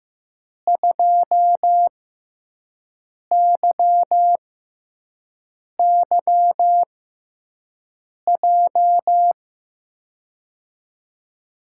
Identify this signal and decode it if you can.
Morse code, heard as a keyed tone, about 15 wpm, spaced as for 5 wpm, 700 Hz. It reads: 2YYJ
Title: USA Call Signs - 6x Character Spacing - 15wpm